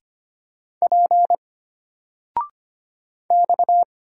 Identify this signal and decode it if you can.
Morse code, heard as a keyed tone, about 25 wpm, 700 Hz. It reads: P X
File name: Single-Letter-Number-25wpm